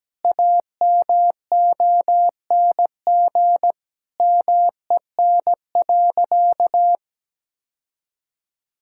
Morse code, heard as a keyed tone, about 17 wpm, 700 Hz. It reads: AMONG MEN.
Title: Sentences from Top 1600 Words 17wpm